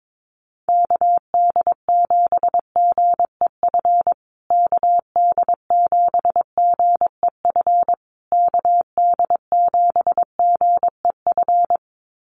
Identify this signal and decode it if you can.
Morse code, heard as a keyed tone, about 22 wpm, 700 Hz. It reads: KD7GEF KD7GEF KD7GEF
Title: USA Call Signs Repeated 3x 22wpm